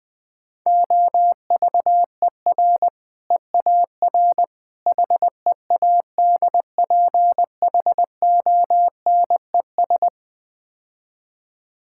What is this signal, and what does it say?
Morse code, heard as a keyed tone, about 20 wpm, 700 Hz. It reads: OVER EAR HEADPHONES
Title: Difficult Words - Speed-Racing with Hints - 20wpm